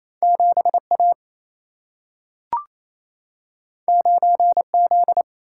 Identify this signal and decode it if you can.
Morse code, heard as a keyed tone, about 28 wpm, 700 Hz. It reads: 7A 9Z
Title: Single-Letter Single-Number 28wpm